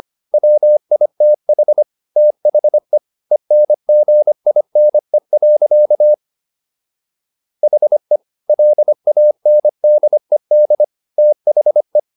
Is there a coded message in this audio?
Yes: Morse code from 0.3 to 12.0 s, heard as a keyed tone, about 25 wpm, 600 Hz. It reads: WITH THE ENGINE. HE LANDED THE